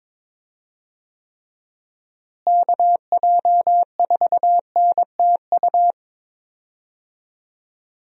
Morse code, heard as a keyed tone, about 22 wpm, 700 Hz. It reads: KJ4NTU
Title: USA Call Signs - 6 Characters - Repeated 3x - 22wpm